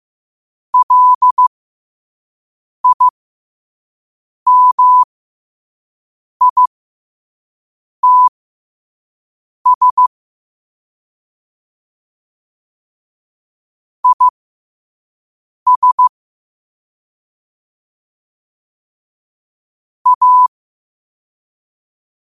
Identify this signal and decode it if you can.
Morse code, heard as a keyed tone, about 15 wpm, spaced as for 5 wpm, 1000 Hz. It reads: LIMITS IS A
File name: qotdfiles / 5wpm